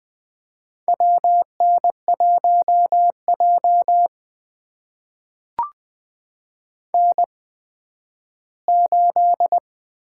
Morse code, heard as a keyed tone, about 20 wpm, spaced as for 12 wpm, 700 Hz. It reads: WN1J N8
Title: USA Call Signs - 8x Character Spacing - 20wpm